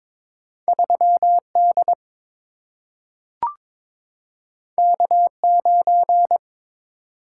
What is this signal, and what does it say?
Morse code, heard as a keyed tone, about 22 wpm, 700 Hz. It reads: 3D K9
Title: Single-Letter Single-Number 22wpm